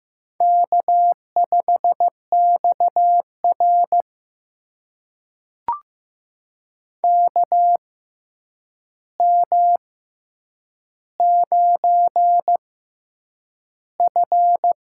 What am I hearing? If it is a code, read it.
Morse code, heard as a keyed tone, about 15 wpm, spaced as for 8 wpm, 700 Hz. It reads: K5XR KM9F
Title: USA Call Signs - 4 Characters - 6x Character Spacing - 15wpm